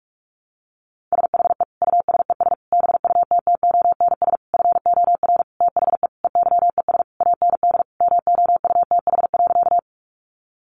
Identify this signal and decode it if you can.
Morse code, heard as a keyed tone, about 44 wpm, 700 Hz. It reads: SHE USES BUTTONS FOR THE EYES AND MOUTH.
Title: Easy Phrases 45wpm